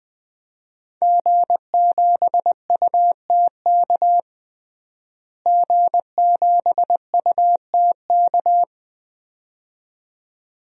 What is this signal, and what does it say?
Morse code, heard as a keyed tone, about 20 wpm, 700 Hz. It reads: G7UTK G7UTK
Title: DXCC-100 Call Signs - 5 Characters - Repeated 2x - 20wpm